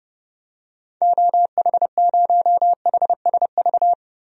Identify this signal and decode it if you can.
Morse code, heard as a keyed tone, about 30 wpm, 700 Hz. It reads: OH0HSV
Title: DXCC-100 Call Signs - 6 Characters - Repeated 2x - 30wpm